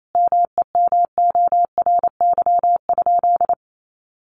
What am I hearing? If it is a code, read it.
Morse code, heard as a keyed tone, about 28 wpm, 700 Hz. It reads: MEMORY?
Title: Sentences from Top 1300 Words 28wpm